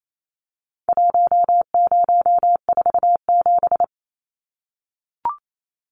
Morse code, heard as a keyed tone, about 28 wpm, 700 Hz. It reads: 1047
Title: Numbers - Contest Style - 4-Digit 28wpm